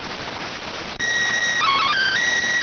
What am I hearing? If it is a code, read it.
SSTV (slow-scan television), PD 290.